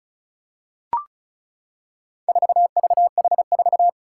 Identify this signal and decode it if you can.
Morse code, heard as a keyed tone, about 34 wpm, 700 Hz. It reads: 4VH4
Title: Four Challenging Letter-Number Combinations 35wpm